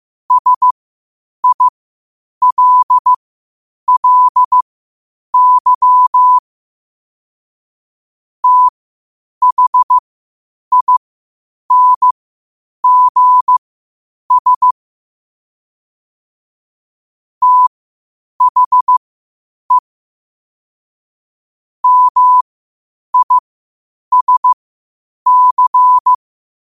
Morse code, heard as a keyed tone, about 15 wpm, spaced as for 8 wpm, 1000 Hz. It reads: SILLY THINGS THE MISC
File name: qotdfiles / 8wpm